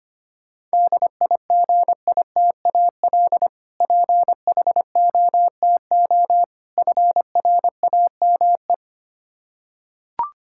Morse code, heard as a keyed tone, about 25 wpm, 700 Hz. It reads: DIGITAL PHOTO FRAME
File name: practice / 25wpm / Difficult Words with Hints - 25wpm